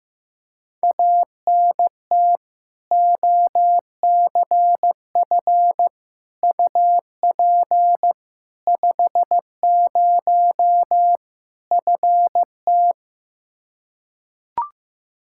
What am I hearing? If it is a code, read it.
Morse code, heard as a keyed tone, about 15 wpm, 700 Hz. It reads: ANT OCF UP 50 FT
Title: Practice QSO - Exchange 2 - Info - 15wpm